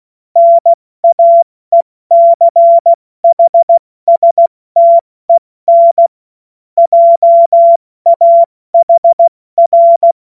Morse code, heard as a keyed tone, about 16 wpm, spaced as for 14 wpm, 680 Hz. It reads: NAECHSTEN JAHR
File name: Morsekurs / Witze / Witz 30. Mai 2023 70 F80